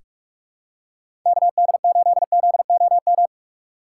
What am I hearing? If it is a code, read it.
Morse code, heard as a keyed tone, about 46 wpm, 700 Hz. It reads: KD8ZOM
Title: USA Call Signs - 6 Characters - 3x Character Spacing - 45wpm